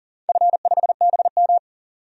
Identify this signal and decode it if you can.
Morse code, heard as a keyed tone, about 38 wpm, 700 Hz. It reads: F5BM
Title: DXCC-100 Call Signs - 4 Characters - 40wpm